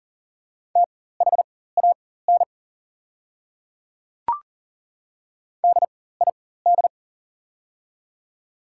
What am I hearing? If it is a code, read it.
Morse code, heard as a keyed tone, about 40 wpm, spaced as for 19 wpm, 700 Hz. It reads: THAN DID
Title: 4x Character Spacing - Top 300 Words - 40wpm